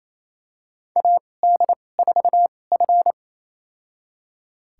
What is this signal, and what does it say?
Morse code, heard as a keyed tone, about 28 wpm, spaced as for 20 wpm, 700 Hz. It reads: AD4F
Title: USA Call Signs - 4 Characters - 2x Character Spacing - 28wpm